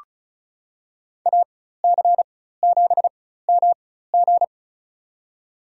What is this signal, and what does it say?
Morse code, heard as a keyed tone, about 36 wpm, spaced as for 16 wpm, 700 Hz. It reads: AC7MG
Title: USA Call Signs - 5 Characters - 4x Character Spacing - 35wpm